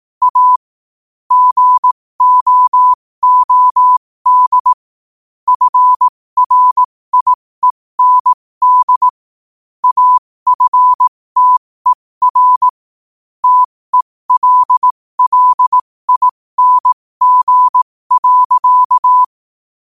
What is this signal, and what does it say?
Morse code, heard as a keyed tone, about 18 wpm, spaced as for 15 wpm, 1000 Hz. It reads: A GOOD FRIEND AFTER TELLING.
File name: qotdfiles / 15wpm